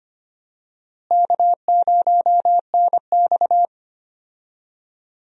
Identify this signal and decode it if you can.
Morse code, heard as a keyed tone, about 25 wpm, 700 Hz. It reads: K0NX
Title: USA Call Signs - 4 Characters - Repeated 2x - 25wpm